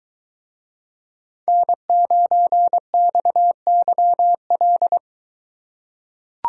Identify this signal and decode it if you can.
Morse code, heard as a keyed tone, about 22 wpm, 700 Hz. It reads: N9XYL